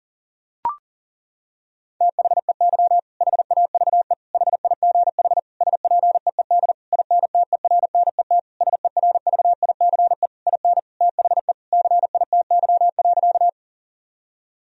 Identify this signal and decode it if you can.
Morse code, heard as a keyed tone, about 40 wpm, 700 Hz. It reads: THEY HAVE HIGH SPEED INTERNET SERVICE IN THE CITY.
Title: Sentences from Top 1000 Words - Review Part 4 - 40wpm